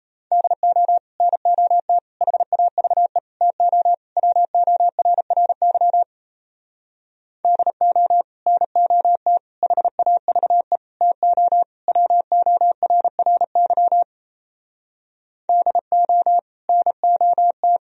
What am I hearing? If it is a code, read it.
Morse code, heard as a keyed tone, about 38 wpm, 700 Hz. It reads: DO NOT HAVE TO WORRY DO NOT HAVE TO WORRY DO NOT